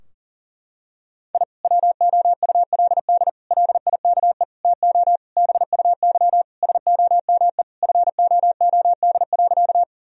Morse code, heard as a keyed tone, about 40 wpm, 700 Hz. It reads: I WOULD LIKE TO BUY SOME FOOD.
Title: Sentences from Top 1000 Words - Review Part 4 - 40wpm